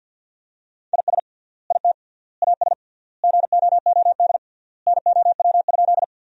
Morse code, heard as a keyed tone, about 50 wpm, 700 Hz. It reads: IS IT AS GOOD NOW?
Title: 3x Word-Spacing - Sentences from Top 200 Words - 50wpm